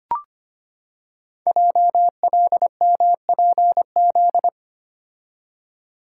Morse code, heard as a keyed tone, about 25 wpm, 700 Hz. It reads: JLMPZ
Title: Code Groups - 5 Letters - 25wpm